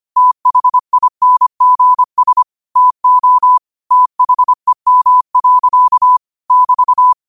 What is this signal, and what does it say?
Morse code, heard as a keyed tone, about 25 wpm, 1000 Hz. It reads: THINGS TO THEM. <BT>
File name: qotdfiles / 25wpm